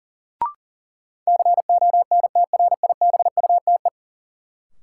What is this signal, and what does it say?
Morse code, heard as a keyed tone, about 40 wpm, 700 Hz. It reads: CONTRIBUTE
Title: Sentences from Top 1400 Words - New Word Drill - 40wpm